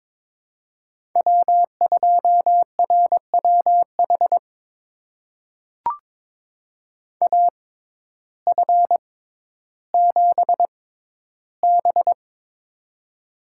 Morse code, heard as a keyed tone, about 21 wpm, spaced as for 12 wpm, 700 Hz. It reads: W2RWH AF7B
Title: USA Call Signs - 6x Character Spacing - 22wpm